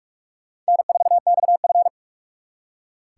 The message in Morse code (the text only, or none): N4XF